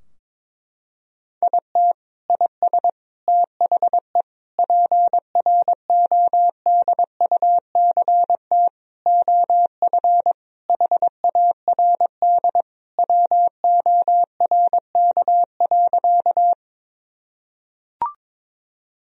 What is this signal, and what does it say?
Morse code, heard as a keyed tone, about 22 wpm, 700 Hz. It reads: IT IS THE PRODUCT OF HARD WORK.
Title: Sentences from Top 1000 Words - Review Part 2 - 22wpm